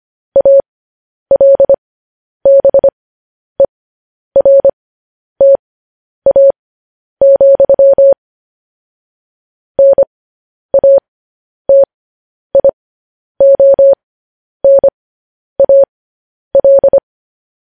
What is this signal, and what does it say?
Morse code, heard as a keyed tone, about 25 wpm, spaced as for 10 wpm, 550 Hz. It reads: ALBERTA, NATIONAL
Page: news headlines in morse code at 10 wpm